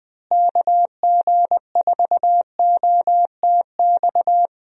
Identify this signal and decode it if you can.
Morse code, heard as a keyed tone, about 20 wpm, 700 Hz. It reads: KG4OTX
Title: USA Call Signs - 6 Characters - Repeated 3x - 20wpm